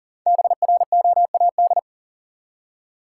Morse code, heard as a keyed tone, about 40 wpm, 700 Hz. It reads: BROAD
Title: Five Letter Words - 40wpm